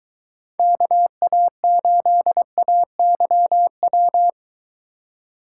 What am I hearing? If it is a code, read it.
Morse code, heard as a keyed tone, about 23 wpm, 700 Hz. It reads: KA8AYW